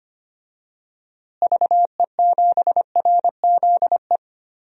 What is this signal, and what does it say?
Morse code, heard as a keyed tone, about 25 wpm, 700 Hz. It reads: VE7RZE
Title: DXCC-100 Call Signs - 6 Characters - Repeated 3x - 25wpm